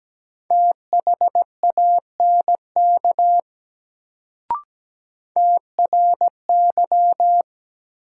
Morse code, heard as a keyed tone, about 17 wpm, 700 Hz. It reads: THANK TRY
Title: Sets of 2 Words - New Word Drill - 17wpm